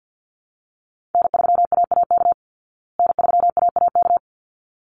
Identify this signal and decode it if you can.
Morse code, heard as a keyed tone, about 48 wpm, 700 Hz. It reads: N3AAK N3AAK
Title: Call Signs Repeated 3x - 50wpm